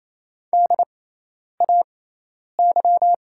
Morse code, heard as a keyed tone, about 28 wpm, spaced as for 10 wpm, 700 Hz. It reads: DAY
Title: 6x Character Spacing - Top 100 Words - 28wpm